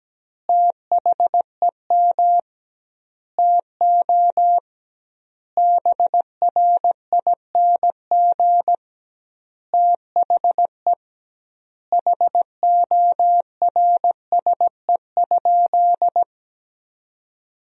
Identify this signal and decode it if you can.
Morse code, heard as a keyed tone, about 17 wpm, 700 Hz. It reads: THEM TO BRING THE HORSE?
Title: 2x Word-Spacing - Sentences from Top 500 Words - 17wpm